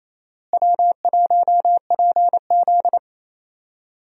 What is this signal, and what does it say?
Morse code, heard as a keyed tone, about 28 wpm, 700 Hz. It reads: W1PZ
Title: USA Call Signs - 4 Characters - Repeated 2x - 28wpm